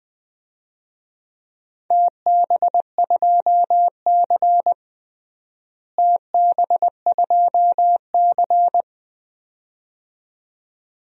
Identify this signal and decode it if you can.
Morse code, heard as a keyed tone, about 20 wpm, 700 Hz. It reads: TB2C TB2C
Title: DXCC-100 Call Signs - 4 Characters - Repeated 2x - 20wpm